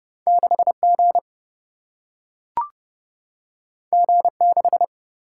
Morse code, heard as a keyed tone, about 31 wpm, 700 Hz. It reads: 6G G6
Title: Single-Letter Single-Number 30wpm